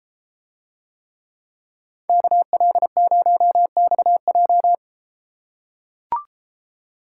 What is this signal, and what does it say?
Morse code, heard as a keyed tone, about 32 wpm, 700 Hz. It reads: KL0XJ